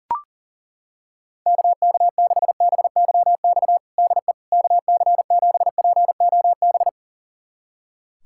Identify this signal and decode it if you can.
Morse code, heard as a keyed tone, about 40 wpm, 700 Hz. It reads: KK6BYX DE KC7POB
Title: USA Callsign DE Callsign - 6 Characters - 40wpm